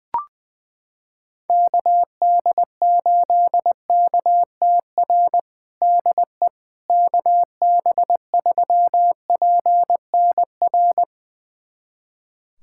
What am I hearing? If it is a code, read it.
Morse code, heard as a keyed tone, about 20 wpm, 700 Hz. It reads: KD8KTR DE KB3PNR
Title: USA Callsign DE Callsign - 6 Characters - 20wpm